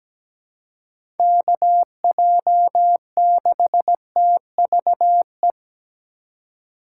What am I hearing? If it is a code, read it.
Morse code, heard as a keyed tone, about 17 wpm, 700 Hz. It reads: KJ6TVE
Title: USA Call Signs - 6 Characters - 4x Character Spacing - 17wpm